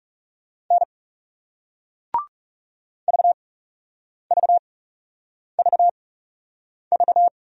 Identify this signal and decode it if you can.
Morse code, heard as a keyed tone, about 41 wpm, spaced as for 8 wpm, 700 Hz. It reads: N VVVV